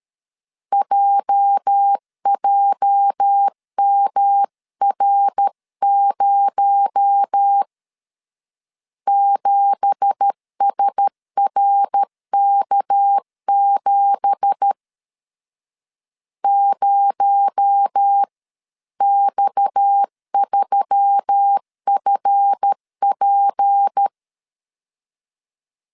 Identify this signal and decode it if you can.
Morse code, heard as a keyed tone, about 13 wpm, 800 Hz. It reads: JJMR0 7SRK7 0X3FP
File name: CW / DET / 1 9 1500C S C 12wpm 3811